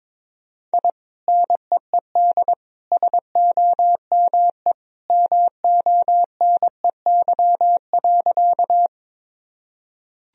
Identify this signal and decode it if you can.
Morse code, heard as a keyed tone, about 22 wpm, 700 Hz. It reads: I NEED SOME MONEY.
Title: Sentences from Top 1000 Words - Review Part 1 - 22wpm